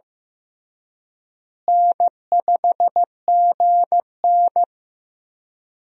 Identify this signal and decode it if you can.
Morse code, heard as a keyed tone, about 16 wpm, 700 Hz. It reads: N5GN
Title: USA Call Signs - 4 Characters - Repeated 3x - 15wpm